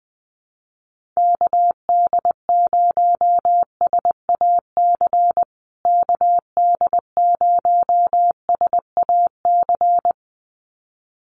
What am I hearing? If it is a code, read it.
Morse code, heard as a keyed tone, about 20 wpm, 700 Hz. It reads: KD0SAC KD0SAC
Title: USA Call Signs Repeated 2x 20wpm